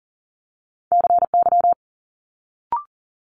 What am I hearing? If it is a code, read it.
Morse code, heard as a keyed tone, about 40 wpm, 700 Hz. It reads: CY
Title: Multi-letter Phonemes Quick Fire 40wpm